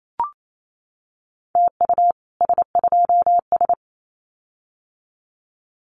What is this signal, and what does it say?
Morse code, heard as a keyed tone, about 28 wpm, 700 Hz. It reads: TU S2S
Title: Common QSO Phrases Encore 28wpm